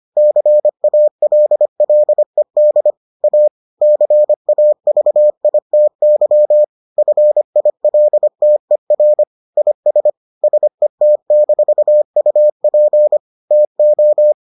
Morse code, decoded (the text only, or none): CALLED A CAVITY FILTER IS SET-UP TO